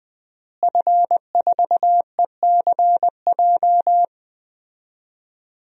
Morse code, heard as a keyed tone, about 20 wpm, 700 Hz. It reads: F4ECJ
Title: DXCC-100 Call Signs - 5 Characters - Repeated 2x - 20wpm